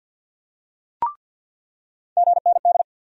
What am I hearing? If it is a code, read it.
Morse code, heard as a keyed tone, about 48 wpm, 700 Hz. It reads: GND
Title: USA Call Sign Postfixes - 3 Letters 50wpm